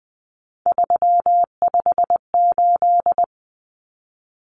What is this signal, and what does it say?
Morse code, heard as a keyed tone, about 20 wpm, 700 Hz. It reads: 358
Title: Numbers - 3-Digit 20wpm